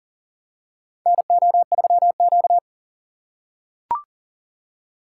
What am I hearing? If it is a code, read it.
Morse code, heard as a keyed tone, about 40 wpm, 700 Hz. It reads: NO3Q
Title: USA Call Signs - 4x Character Spacing - 40wpm